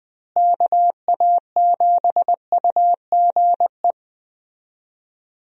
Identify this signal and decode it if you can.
Morse code, heard as a keyed tone, about 20 wpm, 700 Hz. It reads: KA7UGE
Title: USA Call Signs - 6 Characters - Repeated 3x - 20wpm